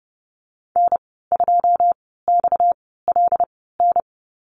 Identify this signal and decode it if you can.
Morse code, heard as a keyed tone, about 30 wpm, spaced as for 17 wpm, 700 Hz. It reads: N2XLN
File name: Lesson 37g - Letter x - Call Signs 30wpm